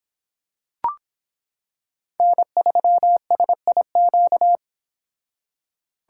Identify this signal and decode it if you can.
Morse code, heard as a keyed tone, about 25 wpm, 700 Hz. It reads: N3SIQ